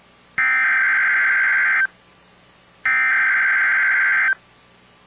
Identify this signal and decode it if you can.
Emergency Alert System (SAME) tones (header bursts), heard as harsh buzzing bursts. It is heard from 0.4 to 4.3 s.